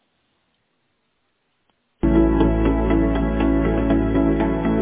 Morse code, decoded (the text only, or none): none